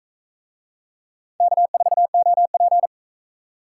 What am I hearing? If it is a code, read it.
Morse code, heard as a keyed tone, about 42 wpm, 700 Hz. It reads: K4OP